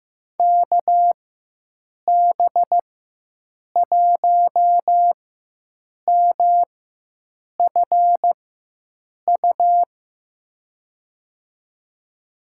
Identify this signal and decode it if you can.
Morse code, heard as a keyed tone, about 15 wpm, spaced as for 7 wpm, 700 Hz. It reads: KB1MFU
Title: USA Call Signs - 6 Characters - 4x Character Spacing - 15wpm